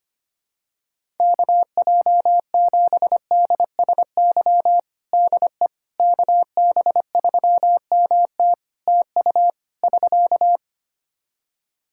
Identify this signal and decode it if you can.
Morse code, heard as a keyed tone, about 25 wpm, 700 Hz. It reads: KJ7DSY DE KB3MT TU <SK>
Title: K1USN SST QSO Practice 25wpm